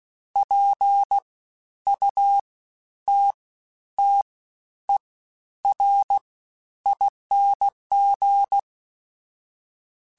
Morse code, heard as a keyed tone, about 16 wpm, spaced as for 10 wpm, 800 Hz. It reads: PUTTERING
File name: Special INGs-mp3s / Special INGs-mp3s / Fast ing @15